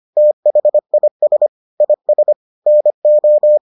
Morse code, heard as a keyed tone, about 25 wpm, 600 Hz. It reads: THIS IS NO